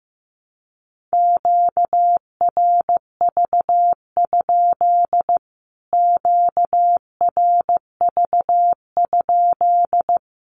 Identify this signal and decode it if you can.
Morse code, heard as a keyed tone, about 15 wpm, 700 Hz. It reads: QRV? QRV?
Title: Q-Codes Repeated 3x 15wpm